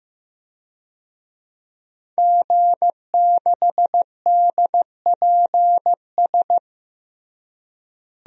Morse code, heard as a keyed tone, about 15 wpm, 700 Hz. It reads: G6DPS
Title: DXCC-100 Call Signs - 5 Characters - Repeated 2x - 15wpm